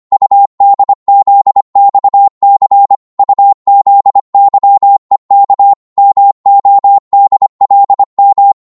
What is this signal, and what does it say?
Morse code, heard as a keyed tone, about 25 wpm, 800 Hz. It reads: UDZXC UZYEK MODLM
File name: Morse / ltr-25wpm